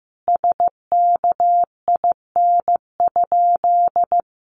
Morse code, heard as a keyed tone, about 15 wpm, 700 Hz. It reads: SKIN?
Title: Sentences from Top 1300 Words 15wpm